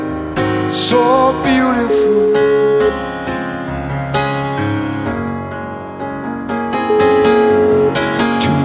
Ringback tone (Europe, 425 Hz), with noise.